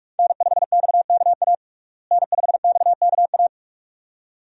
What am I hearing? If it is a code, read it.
Morse code, heard as a keyed tone, about 46 wpm, 700 Hz. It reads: N5XKA N5XKA
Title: USA Call Signs - 5 Characters - Repeated 2x - 45wpm